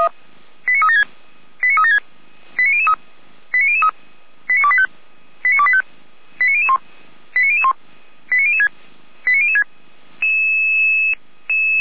Five-tone selective call (ZVEI), heard as quick runs of five tones, tones of about 70 ms. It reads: 89376 89376 89003 89003 89285 89285 89001 89001 89006 89006